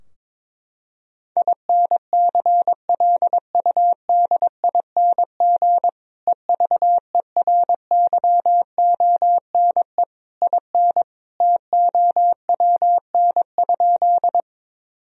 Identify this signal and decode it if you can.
Morse code, heard as a keyed tone, about 22 wpm, 700 Hz. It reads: INCLUDING EVERYONE IN TOWN?
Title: Sentences from Top 1000 Words - Review Part 4 - 22wpm